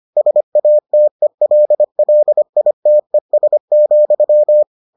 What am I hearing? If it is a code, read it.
Morse code, heard as a keyed tone, about 25 wpm, 600 Hz. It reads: SATELLITES,